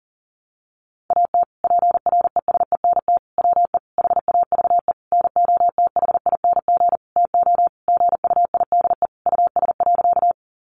Morse code, heard as a keyed tone, about 40 wpm, 700 Hz. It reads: AT PRESENT WE HAVE NOTHING TO GUIDE US.
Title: Sentences from Top 800 Words 40wpm